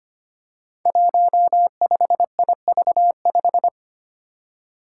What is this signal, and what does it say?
Morse code, heard as a keyed tone, about 25 wpm, 700 Hz. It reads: 15IV5
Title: Five Challenging Letter-Number Combinations 25wpm